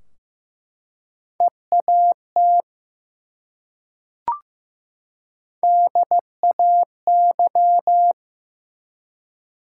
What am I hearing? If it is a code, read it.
Morse code, heard as a keyed tone, about 15 wpm, 700 Hz. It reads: EAT DAY
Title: Three Letter Words - 15wpm